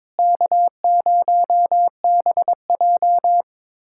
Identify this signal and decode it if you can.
Morse code, heard as a keyed tone, about 21 wpm, 700 Hz. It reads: K0BJ
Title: USA Callsign DE Callsign - 4 Characters - 22wpm